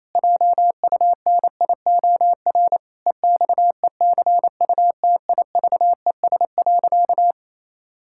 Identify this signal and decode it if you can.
Morse code, heard as a keyed tone, about 28 wpm, 700 Hz. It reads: JUNIOR EXECUTIVES.
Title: Pangram Challenge v2 - 28wpm